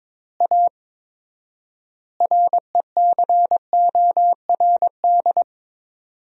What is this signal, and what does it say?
Morse code, heard as a keyed tone, about 22 wpm, 700 Hz. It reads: A RECORD
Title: 4x Word-Spacing - Sentences from Top 500 Words - 22wpm